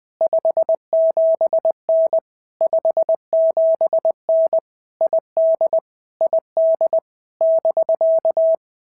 Morse code, heard as a keyed tone, about 20 wpm, 650 Hz. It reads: 57N 57N ID ID <BK>
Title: POTA QSO Practice - 20wpm